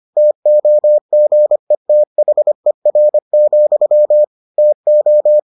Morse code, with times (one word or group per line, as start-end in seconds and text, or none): 0.2-4.2 TOGETHER,
4.6-5.4 TO